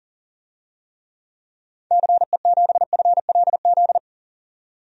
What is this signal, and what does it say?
Morse code, heard as a keyed tone, about 40 wpm, 700 Hz. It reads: CE7FLZ
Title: DXCC-100 Call Signs - 6 Characters - 40wpm